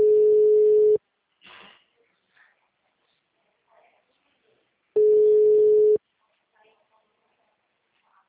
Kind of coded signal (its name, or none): telephone call progress tone